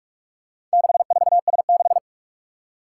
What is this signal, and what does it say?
Morse code, heard as a keyed tone, about 44 wpm, 700 Hz. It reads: 64S6
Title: Four Challenging Letter-Number Combinations 45wpm